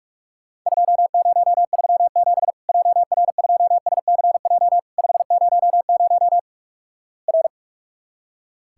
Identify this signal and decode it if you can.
Morse code, heard as a keyed tone, about 46 wpm, 700 Hz. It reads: 2037 JR2SCJ 500 R
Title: Worked All Europe DX Contest - QTC Exchange - 45wpm